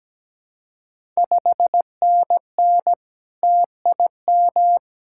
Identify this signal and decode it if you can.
Morse code, heard as a keyed tone, about 17 wpm, 700 Hz. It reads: 5NN TIM
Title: Ontario QSO Party Exchange 17wpm